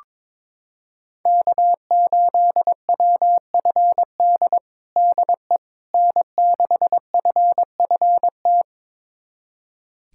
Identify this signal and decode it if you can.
Morse code, heard as a keyed tone, about 22 wpm, 700 Hz. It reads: K8WFD DE N6FFT
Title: USA Callsign DE Callsign - 5 Characters - 22wpm